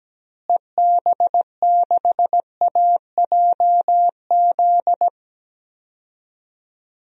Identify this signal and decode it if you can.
Morse code, heard as a keyed tone, about 17 wpm, 700 Hz. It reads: EB6AJZ
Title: DXCC-100 Call Signs - 6 Characters - Repeated 3x - 17wpm